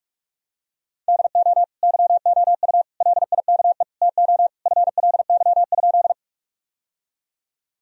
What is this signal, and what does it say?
Morse code, heard as a keyed tone, about 46 wpm, 700 Hz. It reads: DO YOU LIKE TO FLY?
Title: Sentences from Top 1000 Words - Review Part 2 - 45wpm